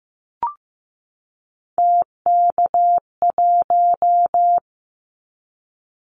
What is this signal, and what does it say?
Morse code, heard as a keyed tone, about 15 wpm, 700 Hz. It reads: TK1
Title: Callsign Prefixes for Top 100 DXCC - 15wpm